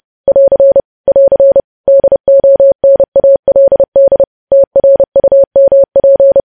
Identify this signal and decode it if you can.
Morse code, heard as a keyed tone, about 30 wpm, 550 Hz. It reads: <AR> <AR> DONALD TRUMP
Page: news headlines in morse code at 30 wpm